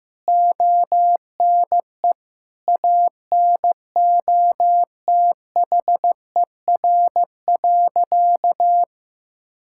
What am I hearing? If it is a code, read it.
Morse code, heard as a keyed tone, about 15 wpm, 700 Hz. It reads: ONE ANOTHER.
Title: Sentences from Top 1600 Words 15wpm